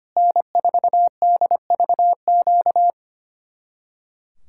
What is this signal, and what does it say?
Morse code, heard as a keyed tone, about 30 wpm, 700 Hz. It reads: N4DVQ
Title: Speed-Racing Common QSO Phrases - 25wpm